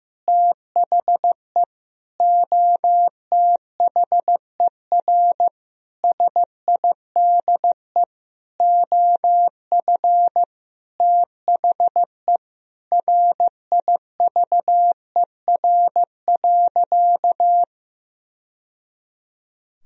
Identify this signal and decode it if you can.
Morse code, heard as a keyed tone, about 15 wpm, 700 Hz. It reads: THE OTHER SIDE OF THE RIVER.
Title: Sentences from Top 1000 Words - Review Part 1 - 15wpm